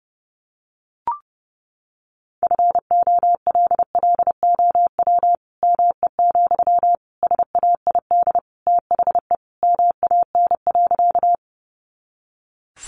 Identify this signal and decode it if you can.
Morse code, heard as a keyed tone, about 30 wpm, 700 Hz. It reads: FOLLOW ME, SAID THE MAN.
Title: Sentences from Top 200 Words 30wpm